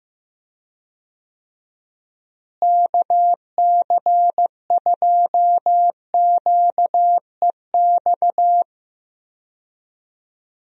Morse code, heard as a keyed tone, about 15 wpm, 700 Hz. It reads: KC2QEX
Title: USA Call Signs - 6 Characters - Repeated 2x - 15wpm